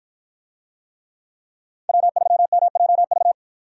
Morse code, heard as a keyed tone, about 53 wpm, 700 Hz. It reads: W3MJV